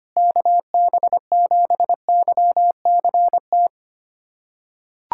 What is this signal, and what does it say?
Morse code, heard as a keyed tone, about 25 wpm, 700 Hz. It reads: KB7YCT
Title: USA Callsign DE Callsign 25wpm